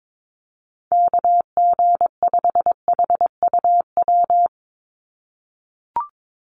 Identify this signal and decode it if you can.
Morse code, heard as a keyed tone, about 22 wpm, 700 Hz. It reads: KG5HUW
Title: USA Call Signs 22wpm